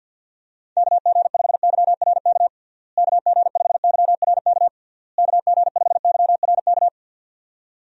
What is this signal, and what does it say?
Morse code, heard as a keyed tone, about 52 wpm, 700 Hz. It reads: KG5YRK KG5YRK KG5YRK
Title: USA Call Signs - 6 Characters - Repeated 3x - 50wpm